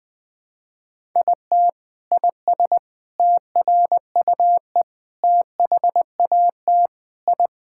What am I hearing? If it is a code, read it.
Morse code, heard as a keyed tone, about 19 wpm, 700 Hz. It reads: IT IS TRUE THAT I
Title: Sentences from Top 1000 Words - Review Part 6 - 20wpm